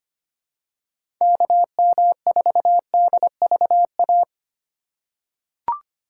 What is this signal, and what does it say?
Morse code, heard as a keyed tone, about 25 wpm, 690 Hz. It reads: KM4DVA
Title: USA Call Signs - 6 Characters - 2x Character Spacing - 25wpm